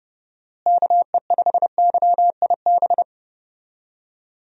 Morse code, heard as a keyed tone, about 29 wpm, 700 Hz. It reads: KE5YIB
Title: USA Call Signs - 6 Characters - Repeated 3x - 30wpm